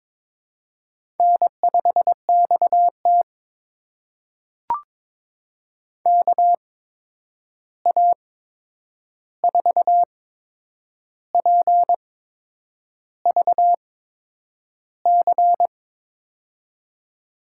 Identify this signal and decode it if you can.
Morse code, heard as a keyed tone, about 23 wpm, spaced as for 8 wpm, 700 Hz. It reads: N5XT KA4PVC
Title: USA Call Signs - 8x Character Spacing - 22wpm